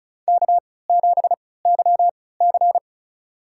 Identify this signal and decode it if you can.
Morse code, heard as a keyed tone, about 36 wpm, spaced as for 20 wpm, 700 Hz. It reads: K7YC
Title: USA Call Signs - 3x Character Spacing - 35wpm